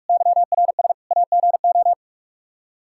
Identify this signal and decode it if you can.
Morse code, heard as a keyed tone, about 46 wpm, 700 Hz. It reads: YRS AGO
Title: Practice QSO - Exchange 3 - More Info - 45wpm